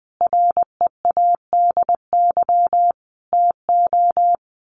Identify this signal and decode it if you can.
Morse code, heard as a keyed tone, about 19 wpm, 700 Hz. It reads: READY TO
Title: Sentences from Top 1400 Words 20wpm